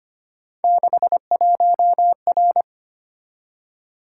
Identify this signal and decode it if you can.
Morse code, heard as a keyed tone, about 25 wpm, 700 Hz. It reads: 61R
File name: Three Letter-Number Combinations - 25wpm